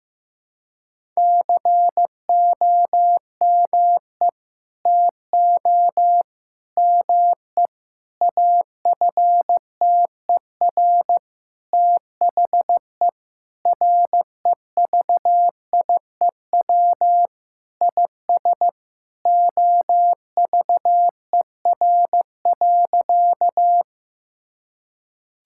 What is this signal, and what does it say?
Morse code, heard as a keyed tone, about 15 wpm, 700 Hz. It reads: COME TO ME AFTER THE REVIEW IS OVER.
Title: Sentences from Top 1000 Words - Review Part 6 - 15wpm